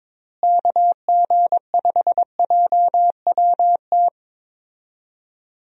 Morse code, heard as a keyed tone, about 21 wpm, 700 Hz. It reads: KG5JWT
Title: USA Call Signs - 6 Characters - Repeated 3x - 22wpm